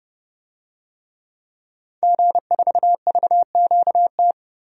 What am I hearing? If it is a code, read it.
Morse code, heard as a keyed tone, about 29 wpm, 700 Hz. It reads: G4VQT
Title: DXCC-100 Call Signs - 5 Characters - 30wpm